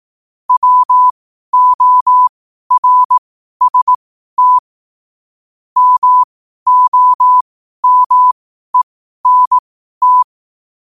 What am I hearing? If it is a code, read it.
Morse code, heard as a keyed tone, about 18 wpm, spaced as for 12 wpm, 1000 Hz. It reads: WORST MOMENT
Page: Quotes for Fri, 15 Aug 2025 in Morse Code at 12 words per minute.